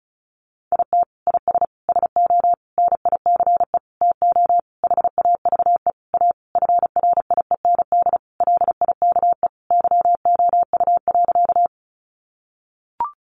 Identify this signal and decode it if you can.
Morse code, heard as a keyed tone, about 34 wpm, 700 Hz. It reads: IT IS SO NICE TO HAVE A FRIEND LIKE YOU.
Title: Sentences from Top 1100 Words 35wpm